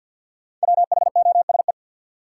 Morse code, heard as a keyed tone, about 52 wpm, 700 Hz. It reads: WHOSE
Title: Sentences from Top 1200 Words - New Word Drill Alt - 50wpm